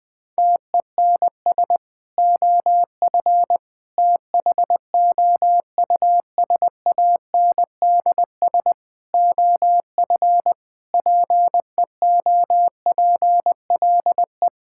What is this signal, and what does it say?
Morse code, heard as a keyed tone, about 21 wpm, 700 Hz. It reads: TENS OF THOUSANDS OF PEOPLE
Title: Sentences with 4-Word Phrases - 20wpm